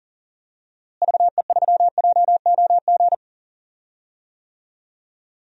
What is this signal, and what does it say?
Morse code, heard as a keyed tone, about 40 wpm, 700 Hz. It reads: VE3JOG
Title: DXCC-100 Call Signs - 6 Characters - Repeated 3x - 40wpm